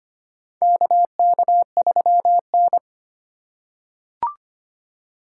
Morse code, heard as a keyed tone, about 25 wpm, 700 Hz. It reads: KK3N
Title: USA Call Signs - 4 Characters - 3x Character Spacing - 25wpm